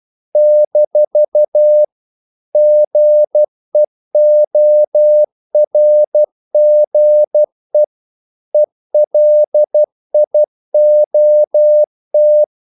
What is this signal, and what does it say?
Morse code, heard as a keyed tone, about 12 wpm, 600 Hz. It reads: - GEORGE ELIOT
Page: English quotations in CW (telegraphy - morse)